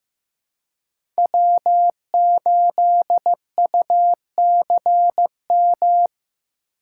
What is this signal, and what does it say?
Morse code, heard as a keyed tone, about 15 wpm, 700 Hz. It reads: W8UCM
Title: USA Call Signs - 5 Characters - 4x Character Spacing - 15wpm